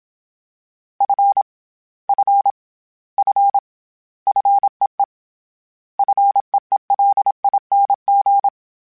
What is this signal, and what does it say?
Morse code, heard as a keyed tone, about 27 wpm, 800 Hz. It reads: F F F FEE FEELING
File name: Short Runs / Short Runs-mp3s / SinglePrefixes @25